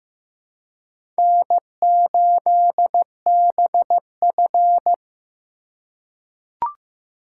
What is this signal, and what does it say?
Morse code, heard as a keyed tone, about 15 wpm, 700 Hz. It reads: N8BF
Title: USA Call Signs - 4 Characters - 4x Character Spacing - 15wpm